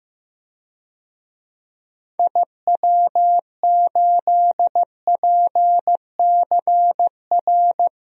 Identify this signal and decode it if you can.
Morse code, heard as a keyed tone, about 15 wpm, 700 Hz. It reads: IW8PCR
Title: DXCC-100 Call Signs - 6 Characters - Repeated 3x - 15wpm